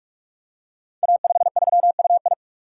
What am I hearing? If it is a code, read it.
Morse code, heard as a keyed tone, about 44 wpm, 700 Hz. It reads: A53UI
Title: Code Groups - 5 Letter:Numbers - 45wpm